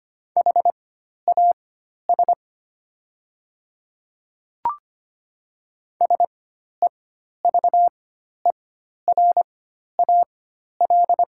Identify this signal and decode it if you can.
Morse code, heard as a keyed tone, about 25 wpm, spaced as for 12 wpm, 690 Hz. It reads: HAS SEVERAL
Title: 4x Character Spacing - Top 500 Words - 25wpm